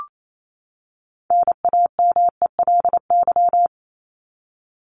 Morse code, heard as a keyed tone, about 28 wpm, 700 Hz. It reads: NAMELY
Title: LY Suffix - Top 100 Words with Base Words 28wpm